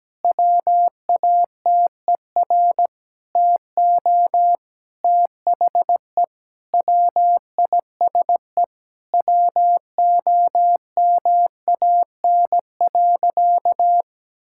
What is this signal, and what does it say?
Morse code, heard as a keyed tone, about 17 wpm, 700 Hz. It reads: WATER TO THE WISE WOMAN.
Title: Sentences from Top 1600 Words 17wpm